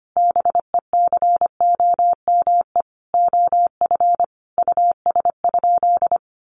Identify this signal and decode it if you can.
Morse code, heard as a keyed tone, about 25 wpm, 700 Hz. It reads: BECOME OF US?
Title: Sentences from Top 600 Words 25wpm